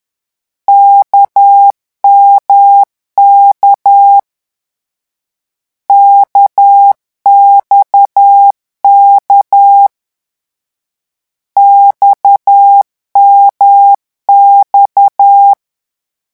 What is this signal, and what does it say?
Morse code, heard as a keyed tone, about 10 wpm, 800 Hz. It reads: KMK KXK XMX
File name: morse mp3 / lecon1-vitesse 10